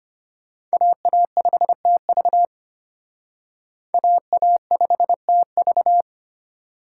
Morse code, heard as a keyed tone, about 32 wpm, 700 Hz. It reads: AA5TV AA5TV